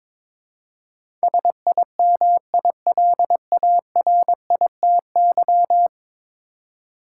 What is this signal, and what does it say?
Morse code, heard as a keyed tone, about 21 wpm, 700 Hz. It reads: SIMILARITY
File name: practice / 22wpm / 10 Letter Words - First 3 Letters Given - 22wpm